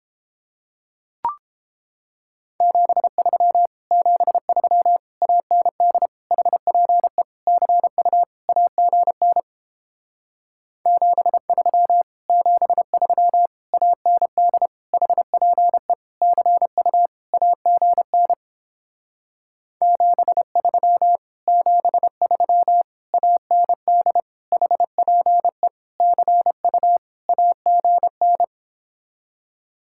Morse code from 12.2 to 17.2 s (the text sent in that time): 73 AND HPE CU